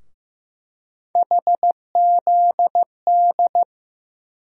Morse code, heard as a keyed tone, about 15 wpm, 700 Hz. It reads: HZD
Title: USA Call Sign Postfixes - 3 Letters 15wpm